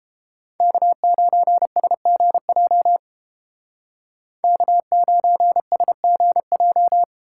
Morse code, heard as a keyed tone, about 33 wpm, 700 Hz. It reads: K9SGJ K9SGJ